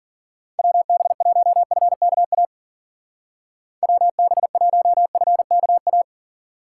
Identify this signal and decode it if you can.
Morse code, heard as a keyed tone, about 51 wpm, 700 Hz. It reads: WB1FKA WB1FKA